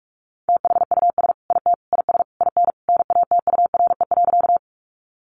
Morse code, heard as a keyed tone, about 44 wpm, 700 Hz. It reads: THUS IT IS IN NATURE.
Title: Sentences from Top 800 Words 45wpm